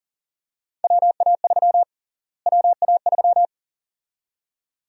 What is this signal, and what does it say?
Morse code, heard as a keyed tone, about 40 wpm, 700 Hz. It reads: WA3 WA3
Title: USA Call Sign Prefixes - Repeated 2x - 40wpm